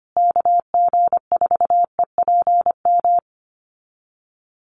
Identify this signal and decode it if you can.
Morse code, heard as a keyed tone, about 25 wpm, 700 Hz. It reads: KG4EPM
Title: USA Call Signs Repeated 2x 25wpm